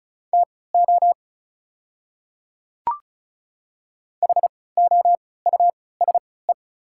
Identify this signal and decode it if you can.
Morse code, heard as a keyed tone, about 34 wpm, spaced as for 20 wpm, 700 Hz. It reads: TO HOUSE
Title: 3x Character Spacing - Top 300 Words - 35wpm